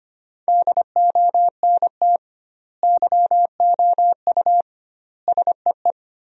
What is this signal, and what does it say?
Morse code, heard as a keyed tone, about 25 wpm, 700 Hz. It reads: DONT YOU SEE
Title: 2x Word-Spacing - Sentences from Top 300 Words - 25wpm